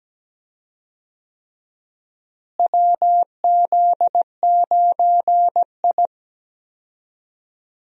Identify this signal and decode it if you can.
Morse code, heard as a keyed tone, about 17 wpm, 700 Hz. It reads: WZ9I
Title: USA Call Signs - 4 Characters - Repeated 3x - 17wpm